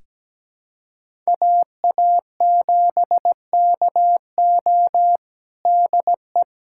Morse code, heard as a keyed tone, about 17 wpm, 700 Hz. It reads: AA7KO DE
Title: USA Callsign DE Callsign - 5 Characters - 17wpm